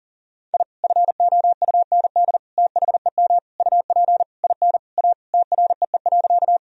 Morse code, heard as a keyed tone, about 40 wpm, 700 Hz. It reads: I FOUND THEM UP IN A TREE.
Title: Sentences from Top 1000 Words - Review Part 1 - 40wpm